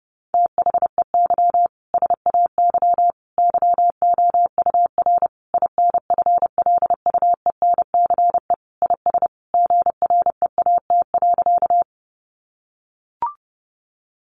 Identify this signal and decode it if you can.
Morse code, heard as a keyed tone, about 30 wpm, 700 Hz. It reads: THEY SAY YOUR INFLUENCE IS GREAT.
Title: Sentences from Top 1100 Words 30wpm